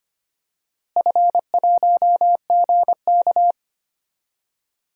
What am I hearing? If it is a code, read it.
Morse code, heard as a keyed tone, about 25 wpm, 700 Hz. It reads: F1GK
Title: DXCC-100 Call Signs - 4 Characters - Repeated 3x - 25wpm